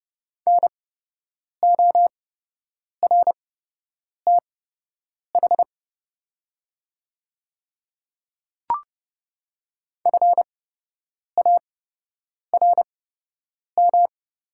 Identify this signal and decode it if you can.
Morse code, heard as a keyed tone, about 30 wpm, spaced as for 8 wpm, 700 Hz. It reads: NORTH FARM
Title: 8x Character Spacing - Top 500 Words - 30wpm